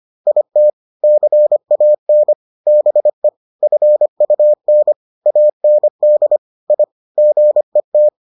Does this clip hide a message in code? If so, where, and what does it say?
Yes: Morse code from 0.3 to 8.1 s, heard as a keyed tone, about 25 wpm, 600 Hz. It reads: IT CAN BE FUN AND I GET